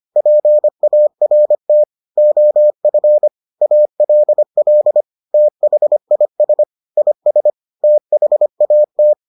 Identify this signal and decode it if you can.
Morse code, heard as a keyed tone, about 25 wpm, 600 Hz. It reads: PART OF ALL THIS IS THAT